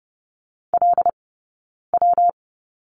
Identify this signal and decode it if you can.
Morse code, heard as a keyed tone, about 30 wpm, spaced as for 9 wpm, 690 Hz. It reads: LW
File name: Lesson 19d - Letter w - Two Characters 30wpm